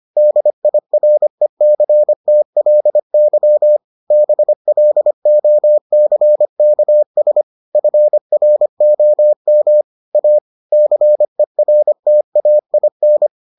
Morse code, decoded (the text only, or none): DIRECTLY BLOCKS FROM A CERTAIN